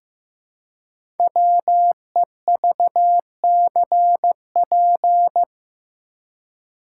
Morse code, heard as a keyed tone, about 15 wpm, 700 Hz. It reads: WEVCP
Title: Code Groups - 5 Letter:Numbers - 15wpm